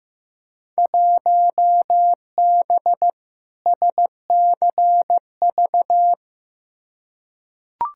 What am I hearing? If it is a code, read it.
Morse code, heard as a keyed tone, about 15 wpm, 700 Hz. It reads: 1B SCV
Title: Field Day Exchanges - 15wpm